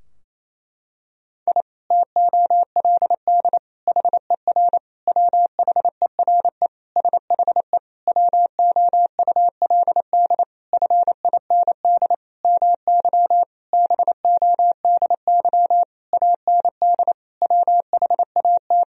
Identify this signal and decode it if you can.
Morse code, heard as a keyed tone, about 28 wpm, 700 Hz. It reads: I TOLD HER WHERE SHE WOULD FIND MY BODY AND WHAT
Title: Sentences from Top 1000 Words - Review Part 2 - 28wpm